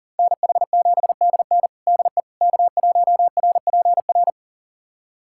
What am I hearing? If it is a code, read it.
Morse code, heard as a keyed tone, about 37 wpm, 700 Hz. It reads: NH7DN DE K1RPR
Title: USA Callsign DE Callsign 40wpm